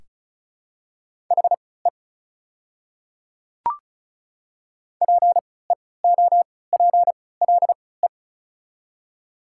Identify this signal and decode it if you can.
Morse code, heard as a keyed tone, about 36 wpm, spaced as for 20 wpm, 690 Hz. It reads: HE PEOPLE
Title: 3x Character Spacing - Top 100 Words - 35wpm